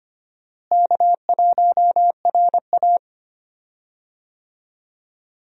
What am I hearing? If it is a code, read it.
Morse code, heard as a keyed tone, about 25 wpm, 700 Hz. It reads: K1RA
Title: USA Call Signs - 4 Characters - Repeated 2x - 25wpm